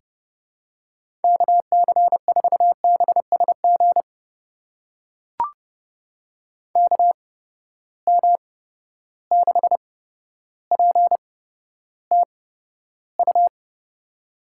Morse code, heard as a keyed tone, about 30 wpm, spaced as for 13 wpm, 700 Hz. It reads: KC4BSG KM6PTU
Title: USA Call Signs - 6 Characters - 8x Character Spacing - 30wpm